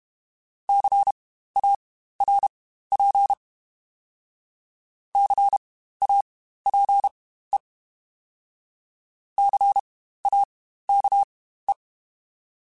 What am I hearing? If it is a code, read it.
Morse code, heard as a keyed tone, about 32 wpm, spaced as for 15 wpm, 800 Hz. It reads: CARP CAPE CAKE
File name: Spaced Alphas / Spaced Alphas mp3s / Cs sp @30